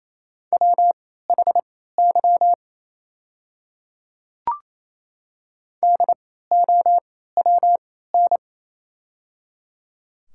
Morse code, heard as a keyed tone, about 28 wpm, spaced as for 16 wpm, 700 Hz. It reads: WHY DOWN
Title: 3x Character Spacing - Top 200 Words - 28wpm